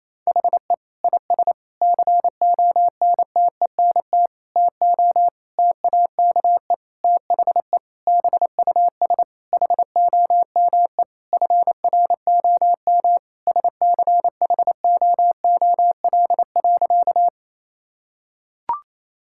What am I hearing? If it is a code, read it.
Morse code, heard as a keyed tone, about 28 wpm, 700 Hz. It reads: HE IS CONTENT TO TAKE THE BUS HOME FROM SCHOOL.
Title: Sentences from Top 1000 Words - Review Part 6 - 28wpm